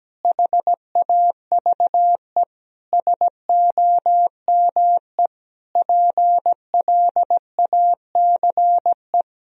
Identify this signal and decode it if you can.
Morse code, heard as a keyed tone, about 17 wpm, 700 Hz. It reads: HAVE SOME PLACE
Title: Sentences from Top 1000 Words - Review Part 3 - 17wpm